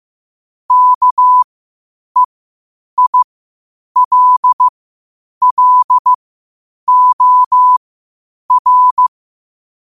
Morse code, heard as a keyed tone, about 15 wpm, spaced as for 8 wpm, 1000 Hz. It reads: KEILLOR
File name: qotdfiles / 8wpm